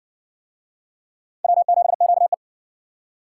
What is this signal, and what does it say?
Morse code, heard as a keyed tone, about 60 wpm, 700 Hz. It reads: W7QE